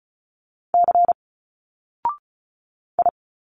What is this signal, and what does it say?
Morse code, heard as a keyed tone, about 36 wpm, 700 Hz. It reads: C I
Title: Single Letters 35wpm